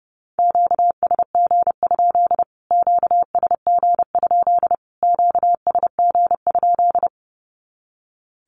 Morse code, heard as a keyed tone, about 30 wpm, 700 Hz. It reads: QSG? QSG? QSG?
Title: Q-Codes Repeated 3x 30wpm